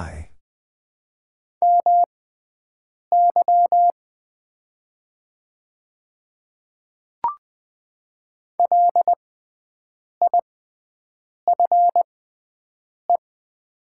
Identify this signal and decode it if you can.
Morse code, heard as a keyed tone, about 20 wpm, spaced as for 7 wpm, 690 Hz. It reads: MY LIFE
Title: 6x Character Spacing - Top 300 Words - 20wpm